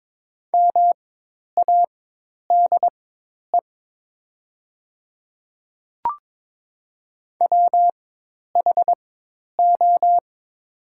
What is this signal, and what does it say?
Morse code, heard as a keyed tone, about 22 wpm, spaced as for 10 wpm, 700 Hz. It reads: MADE WHO
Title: 4x Character Spacing - Top 200 Words - 22wpm